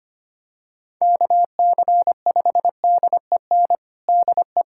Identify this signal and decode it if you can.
Morse code, heard as a keyed tone, about 25 wpm, 700 Hz. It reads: KC5DEN DE
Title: USA Callsign DE Callsign - 6 Characters - 25wpm